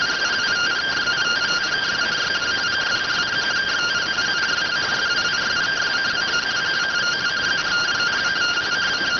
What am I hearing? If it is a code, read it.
Two-tone teleprinter signal, about 170 Hz shift.